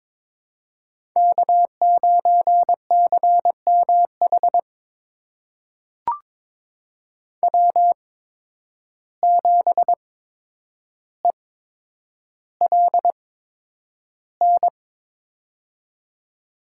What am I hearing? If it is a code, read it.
Morse code, heard as a keyed tone, about 21 wpm, spaced as for 9 wpm, 700 Hz. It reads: K9CMH W7ELN
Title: USA Call Signs - 5 Characters - 8x Character Spacing - 22wpm